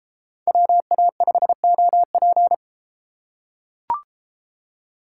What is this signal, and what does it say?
Morse code, heard as a keyed tone, about 32 wpm, 700 Hz. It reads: WA5OP